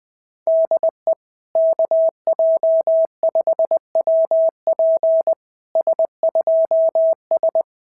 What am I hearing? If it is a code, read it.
Morse code, heard as a keyed tone, about 20 wpm, 660 Hz. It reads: DE KJ5WP S2S
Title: SOTA QSO Practice 20wpm